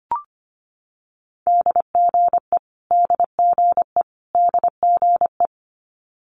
Morse code, heard as a keyed tone, about 25 wpm, 700 Hz. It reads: DGE DGE DGE
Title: Multi-letter Phonemes Repeated 3x 25wpm